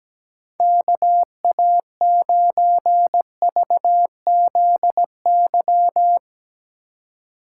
Morse code, heard as a keyed tone, about 17 wpm, 700 Hz. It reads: KA9VZY
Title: Speed Racing - USA 2x3 Call Signs 17wpm